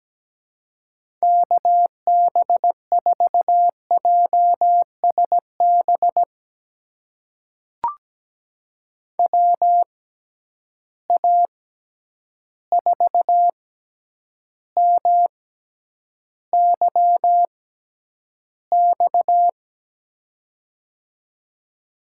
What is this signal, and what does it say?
Morse code, heard as a keyed tone, about 17 wpm, spaced as for 9 wpm, 700 Hz. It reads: KB4JSB WA4MYX
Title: USA Call Signs - 6 Characters - 6x Character Spacing - 17wpm